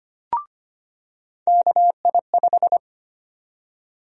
Morse code, heard as a keyed tone, about 25 wpm, 700 Hz. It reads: KI5
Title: USA Call Sign Prefixes - 25wpm